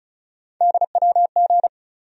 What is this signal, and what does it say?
Morse code, heard as a keyed tone, about 36 wpm, 700 Hz. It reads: DWG
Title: USA Call Sign Postfixes - 3 Letters 35wpm